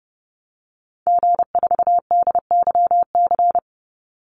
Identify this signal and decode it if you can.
Morse code, heard as a keyed tone, about 30 wpm, 700 Hz. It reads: G4DYC
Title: Call Signs Repeated 3x - 30wpm